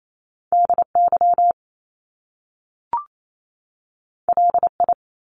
Morse code, heard as a keyed tone, about 28 wpm, 700 Hz. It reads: DY LI
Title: Top Two Letter Combinations in Words 28wpm